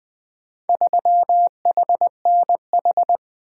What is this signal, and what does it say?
Morse code, heard as a keyed tone, about 20 wpm, 700 Hz. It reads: 3HNH
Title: Four Letter-Number Combinations - 20wpm